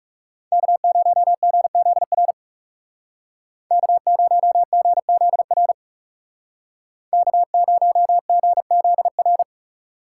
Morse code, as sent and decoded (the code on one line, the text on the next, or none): -.- ----- --. --.. .-. / -.- ----- --. --.. .-. / -.- ----- --. --.. .-.
K0GZR K0GZR K0GZR